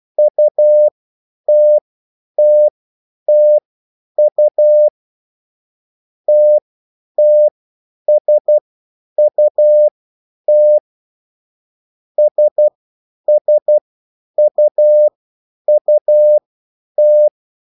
Morse code, heard as a keyed tone, about 12 wpm, spaced as for 9 wpm, 600 Hz. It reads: UTTTU TTSUT SSUUT
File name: QSL / CW lesson